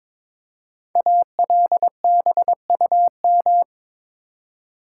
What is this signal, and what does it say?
Morse code, heard as a keyed tone, about 22 wpm, 690 Hz. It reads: ALBUM
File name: 5 Letter Words - First 3 Letters Given - 22wpm